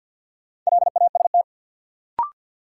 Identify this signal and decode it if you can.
Morse code, heard as a keyed tone, about 51 wpm, 700 Hz. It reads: LAST
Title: Sets of 2 Words - New Word Drill - 50wpm